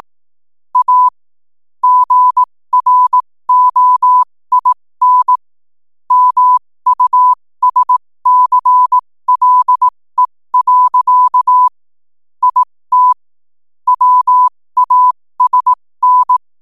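Morse code, heard as a keyed tone, about 18 wpm, spaced as for 15 wpm, 1000 Hz. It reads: A GROIN MUSCLE. IT WASN
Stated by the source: 15 wpm Morse code from RNZN Communicators Association.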